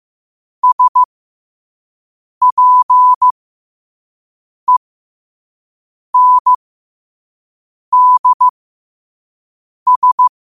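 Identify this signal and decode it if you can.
Morse code, heard as a keyed tone, about 15 wpm, spaced as for 5 wpm, 1000 Hz. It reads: SPENDS
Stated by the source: Morse code sent at 5 wpm for learning.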